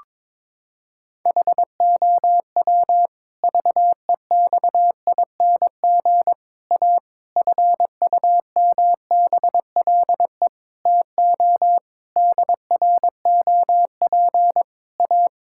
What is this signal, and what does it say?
Morse code, heard as a keyed tone, about 22 wpm, 700 Hz. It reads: HOW VEXING A FUMBLE TO DROP A
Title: Pangram Challenge v2 - 22wpm